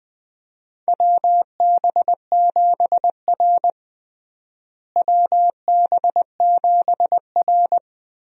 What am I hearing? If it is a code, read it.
Morse code, heard as a keyed tone, about 20 wpm, 700 Hz. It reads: WB7R WB7R
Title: USA Call Signs - 4 Characters - Repeated 2x - 20wpm